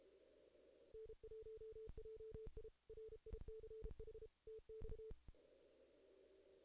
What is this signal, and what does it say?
Morse code, heard as a keyed tone, about 34 wpm, 440 Hz. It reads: N1JI RICH TX
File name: 2016 JAN CW NAQP / 2016 Jan naqp k5ot - 0-11